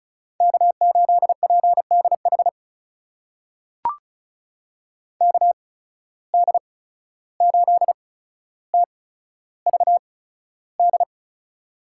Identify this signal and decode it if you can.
Morse code, heard as a keyed tone, about 36 wpm, spaced as for 14 wpm, 700 Hz. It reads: K8PDH KD8TVD
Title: USA Call Signs - 8x Character Spacing - 35wpm